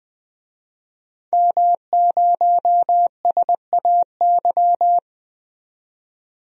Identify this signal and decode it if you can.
Morse code, heard as a keyed tone, about 20 wpm, 700 Hz. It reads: M0SAY
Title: DXCC-100 Call Signs - 5 Characters - 20wpm